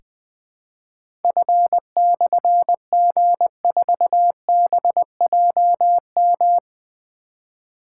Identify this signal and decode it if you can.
Morse code, heard as a keyed tone, about 20 wpm, 700 Hz. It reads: F/G4BJM
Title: Difficult Call Signs 20wpm